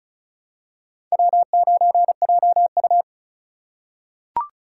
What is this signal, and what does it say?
Morse code, heard as a keyed tone, about 36 wpm, 700 Hz. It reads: W9JU
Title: USA Call Signs - 2x Character Spacing - 35wpm